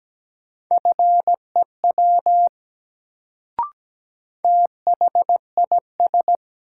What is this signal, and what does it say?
Morse code, heard as a keyed tone, about 17 wpm, 700 Hz. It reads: FEW THIS
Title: Sets of 2 Words - New Word Drill - 17wpm